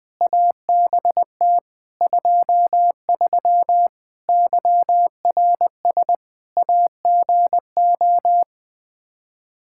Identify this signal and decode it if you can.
Morse code, heard as a keyed tone, about 21 wpm, 700 Hz. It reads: ABT 23 YRS AGO
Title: K1USN SST QSO Practice 20wpm